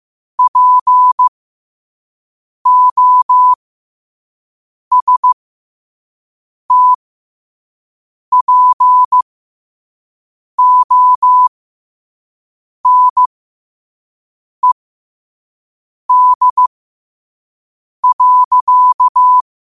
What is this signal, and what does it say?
Morse code, heard as a keyed tone, about 15 wpm, spaced as for 5 wpm, 1000 Hz. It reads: POSTPONED.
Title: 5 WPM morse code quotes for Fri, 01 Mar 2024